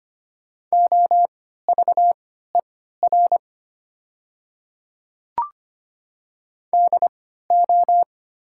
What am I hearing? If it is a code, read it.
Morse code, heard as a keyed tone, about 25 wpm, spaced as for 14 wpm, 700 Hz. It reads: OVER DO
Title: 3x Character Spacing - Top 300 Words - 25wpm